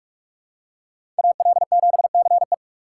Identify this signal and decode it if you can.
Morse code, heard as a keyed tone, about 46 wpm, 700 Hz. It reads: AL7CE
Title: USA Call Signs - 2x Character Spacing - 45wpm